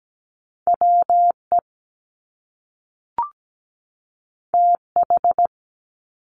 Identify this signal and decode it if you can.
Morse code, heard as a keyed tone, about 17 wpm, 700 Hz. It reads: WE TH
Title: Top Two Letter Combinations in Words 17wpm